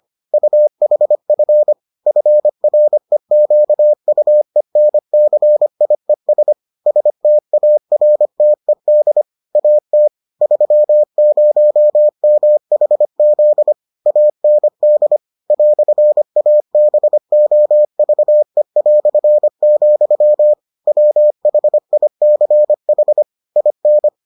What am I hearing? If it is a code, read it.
Morse code, heard as a keyed tone, about 25 wpm, 600 Hz. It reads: UHF FREQUENCIES STARTED AT 30MHZ AND "ABOVE", WHICH IN